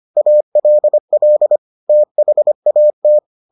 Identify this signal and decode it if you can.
Morse code, heard as a keyed tone, about 25 wpm, 600 Hz. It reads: ALL THAT